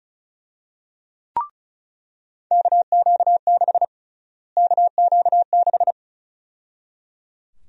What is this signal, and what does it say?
Morse code, heard as a keyed tone, about 36 wpm, 700 Hz. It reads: KQ6 KQ6
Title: USA Call Sign Prefixes - Repeated 2x - 35wpm